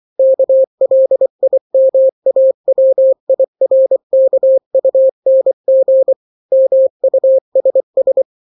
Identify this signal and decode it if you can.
Morse code, heard as a keyed tone, about 24 wpm, 530 Hz. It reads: KLIMAWIRKUNG MUSS
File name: daily / 23wpm